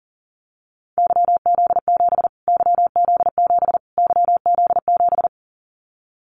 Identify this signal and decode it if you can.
Morse code, heard as a keyed tone, about 37 wpm, 700 Hz. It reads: YZ7 YZ7 YZ7
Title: Callsign Prefixes for Top 25 DXCC - Repeated 3x - 40wpm